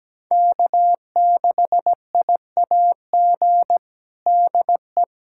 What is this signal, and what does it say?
Morse code, heard as a keyed tone, about 17 wpm, 700 Hz. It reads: K6IAG DE
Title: USA Callsign DE Callsign - 5 Characters - 17wpm